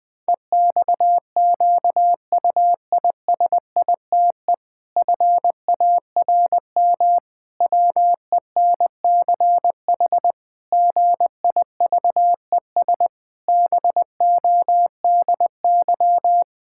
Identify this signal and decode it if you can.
Morse code, heard as a keyed tone, about 19 wpm, 700 Hz. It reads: EXQUISITE FARM WENCH GIVES BODY
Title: Pangram Challenge v2 - 20wpm